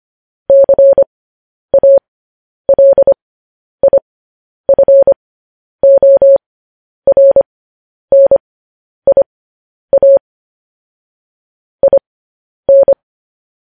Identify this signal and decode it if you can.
Morse code, heard as a keyed tone, about 25 wpm, spaced as for 10 wpm, 550 Hz. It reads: CALIFORNIA IN